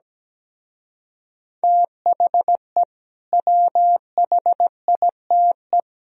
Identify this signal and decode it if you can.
Morse code, heard as a keyed tone, about 17 wpm, 700 Hz. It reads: THE WHITE
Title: Speed-Racing - Sets of 3 Words 17wpm